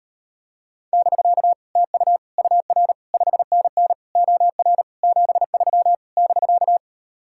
Morse code, heard as a keyed tone, about 38 wpm, 700 Hz. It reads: <BK> TU UR 5NN OR 73 <BK>